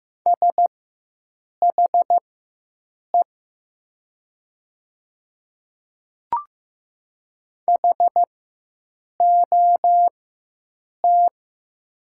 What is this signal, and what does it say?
Morse code, heard as a keyed tone, about 15 wpm, spaced as for 7 wpm, 700 Hz. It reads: SHE HOT
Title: 4x Character Spacing - Top 100 Words - 15wpm